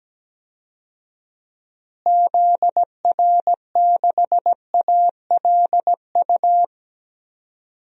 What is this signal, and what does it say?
Morse code, heard as a keyed tone, about 17 wpm, 700 Hz. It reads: ZR6ALU
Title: DXCC-100 Call Signs - 6 Characters - Repeated 3x - 17wpm